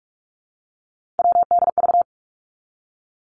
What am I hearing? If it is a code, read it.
Morse code, heard as a keyed tone, about 46 wpm, 700 Hz. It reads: WDV
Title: Three Challenging Letter-Number Combinations 45wpm